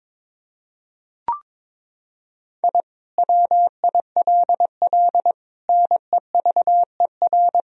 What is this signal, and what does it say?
Morse code, heard as a keyed tone, about 22 wpm, 700 Hz. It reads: I WILL NEVER
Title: Sentences from Top 1000 Words - Review Part 1 - 22wpm